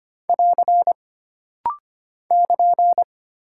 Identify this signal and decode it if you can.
Morse code, heard as a keyed tone, about 25 wpm, 700 Hz. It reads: <AR> <KN>
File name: Single Punctuation-Prosign - Rapid-Fire - 25wpm